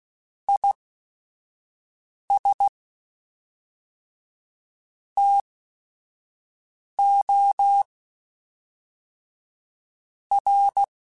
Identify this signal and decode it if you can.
Morse code, heard as a keyed tone, about 16 wpm, spaced as for 5 wpm, 800 Hz. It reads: IS TO R
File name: Technical Subjects / Technical Subjects-mp3s / Sw Pwr Sup @5